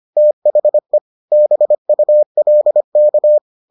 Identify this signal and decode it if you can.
Morse code, heard as a keyed tone, about 25 wpm, 600 Hz. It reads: THE BULK